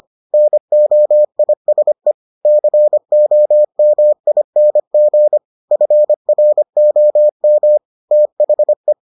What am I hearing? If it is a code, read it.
Morse code, heard as a keyed tone, about 25 wpm, 600 Hz. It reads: NOISE COMING FROM THE